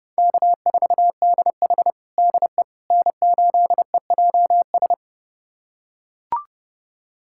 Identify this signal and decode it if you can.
Morse code, heard as a keyed tone, about 30 wpm, 700 Hz. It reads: K4DH DE N8EJS
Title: USA Callsign DE Callsign 30wpm